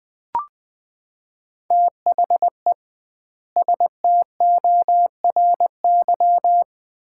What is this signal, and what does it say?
Morse code, heard as a keyed tone, about 20 wpm, 700 Hz. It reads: THE STORY
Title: 2x Word-Spacing - Sets of 3 Words - 20wpm